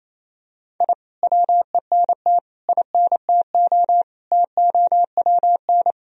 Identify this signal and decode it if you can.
Morse code, heard as a keyed tone, about 28 wpm, 700 Hz. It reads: I WENT INTO TOWN
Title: Sentences from Top 1000 Words - Review Part 4 - 28wpm